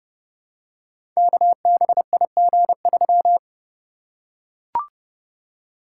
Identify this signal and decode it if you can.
Morse code, heard as a keyed tone, about 28 wpm, 700 Hz. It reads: KBIG3
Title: Code Groups - 5 Letter:Numbers - 30wpm